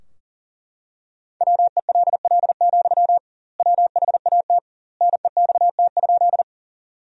Morse code, heard as a keyed tone, about 40 wpm, 700 Hz. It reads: WELL, WHAT NEXT?
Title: 2x Word-Spacing - Sentences from Top 300 Words - 40wpm